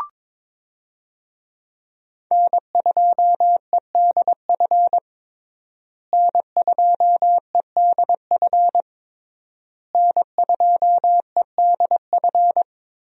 Morse code, heard as a keyed tone, about 22 wpm, 700 Hz. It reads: N2EDF N2EDF N2EDF
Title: USA Call Signs - 5 Characters - Repeated 3x - 22wpm